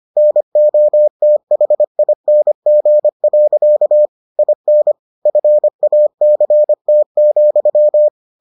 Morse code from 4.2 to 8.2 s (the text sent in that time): IN FACT,